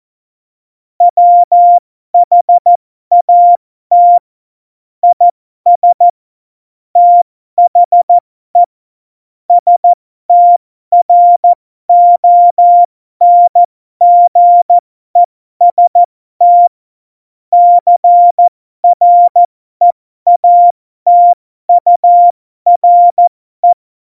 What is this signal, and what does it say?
Morse code, heard as a keyed tone, about 14 wpm, spaced as for 12 wpm, 700 Hz. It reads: WHAT IS THE STRONGEST CREATURE
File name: Morsekurs / Witze